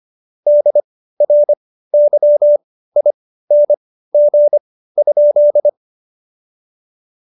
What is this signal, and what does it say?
Morse code, heard as a keyed tone, about 25 wpm, spaced as for 15 wpm, 590 Hz. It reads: DRYING?
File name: Jokes-25-char-15-eff